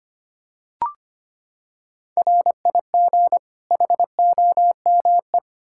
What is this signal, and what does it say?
Morse code, heard as a keyed tone, about 25 wpm, 700 Hz. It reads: RIG HOME
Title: Practice QSO - Exchange 2 - Info - 25wpm